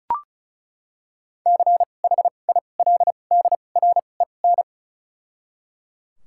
Morse code, heard as a keyed tone, about 36 wpm, spaced as for 25 wpm, 700 Hz. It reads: CHILDREN
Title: 2x Character Spacing - Top 500 Words - 35wpm